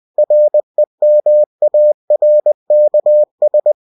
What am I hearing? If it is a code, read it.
Morse code, heard as a keyed tone, about 20 wpm, 600 Hz. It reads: REMARKS
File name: cw-books / winnie-the-pooh / winnie-20wpm-page-0109